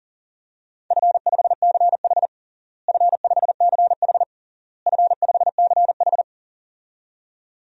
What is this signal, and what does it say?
Morse code, heard as a keyed tone, about 40 wpm, 700 Hz. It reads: F5CH F5CH F5CH
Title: DXCC-100 Call Signs - 4 Characters - Repeated 3x - 40wpm